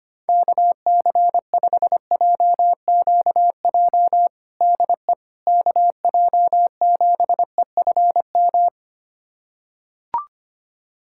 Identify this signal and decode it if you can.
Morse code, heard as a keyed tone, about 25 wpm, 700 Hz. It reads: KC5JQJ DE KJ7EFM
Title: USA Callsign DE Callsign - 6 Characters - 25wpm